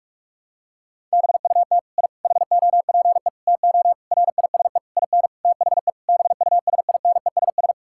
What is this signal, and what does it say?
Morse code, heard as a keyed tone, about 41 wpm, 700 Hz. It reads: BUT I HOPE TO RISE IN THE BUSINESS
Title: Sentences from Top 1000 Words - Review Part 6 - 45wpm